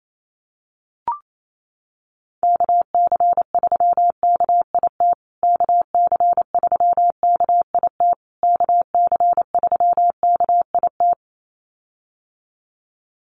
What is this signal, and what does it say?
Morse code, heard as a keyed tone, about 28 wpm, 700 Hz. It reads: KC3KIT KC3KIT KC3KIT
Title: USA Call Signs Repeated 3x 28wpm